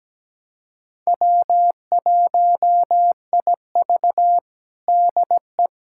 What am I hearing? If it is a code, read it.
Morse code, heard as a keyed tone, about 17 wpm, 700 Hz. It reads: W1IV DE
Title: USA Callsign DE Callsign - 4 Characters - 17wpm